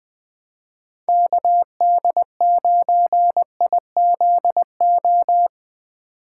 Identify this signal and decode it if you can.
Morse code, heard as a keyed tone, about 20 wpm, 700 Hz. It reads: KD9IZO
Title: Speed Racing - USA 2x3 Call Signs 17wpm